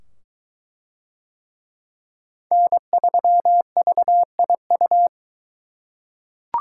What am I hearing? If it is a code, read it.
Morse code, heard as a keyed tone, about 21 wpm, 700 Hz. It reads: N3VIU